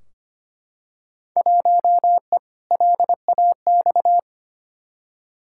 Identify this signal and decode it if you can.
Morse code, heard as a keyed tone, about 25 wpm, 700 Hz. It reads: 1E LAX
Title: Field Day Exchanges - 25wpm